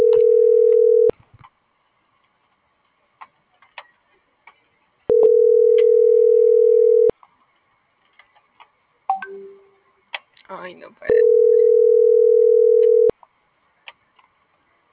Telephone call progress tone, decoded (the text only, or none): ringback tone